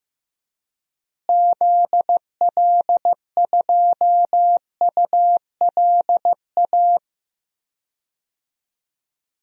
Morse code, heard as a keyed tone, about 15 wpm, 700 Hz. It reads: ZL2ULA
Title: DXCC-100 Call Signs - 6 Characters - Repeated 2x - 15wpm